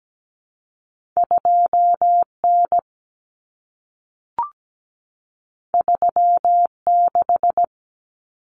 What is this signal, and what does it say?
Morse code, heard as a keyed tone, about 17 wpm, 700 Hz. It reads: 2N 36
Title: Common Cut Numbers - 2-Digit 17wpm